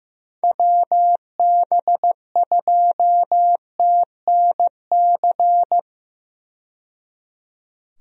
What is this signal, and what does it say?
Morse code, heard as a keyed tone, about 15 wpm, 700 Hz. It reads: WB2TNC
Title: USA Call Signs - 6 Characters - 15wpm